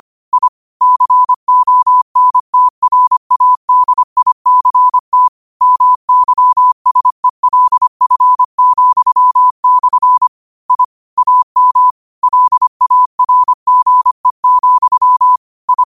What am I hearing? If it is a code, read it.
Morse code, heard as a keyed tone, about 25 wpm, 1000 Hz. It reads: I CONTRADICT MYSELF,/ I AM LARGE, I